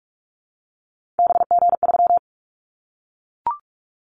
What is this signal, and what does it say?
Morse code, heard as a keyed tone, about 44 wpm, 700 Hz. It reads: BG3
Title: Callsign Prefixes for Top 100 DXCC - 45wpm